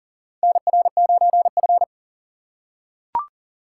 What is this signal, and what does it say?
Morse code, heard as a keyed tone, about 40 wpm, 700 Hz. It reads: NR9F
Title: USA Call Signs - 4 Characters - 3x Character Spacing - 40wpm